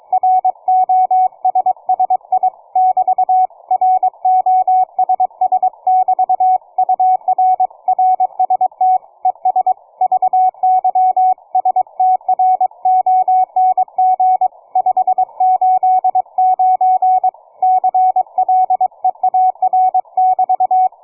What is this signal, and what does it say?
Morse code, heard as a keyed tone, about 22 wpm, 740 Hz, with radio noise. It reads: ROSSI <BT> ROSS<BT> UR RST ES VY STRONG 589 CLEAR<BT>
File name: cw forum / 9m2ax 160m1829 22h14